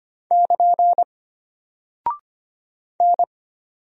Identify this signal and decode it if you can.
Morse code, heard as a keyed tone, about 25 wpm, 700 Hz. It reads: <KN> N
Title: Single Character (Letter:Number:Punctuation:Prosign) Alt 25wpm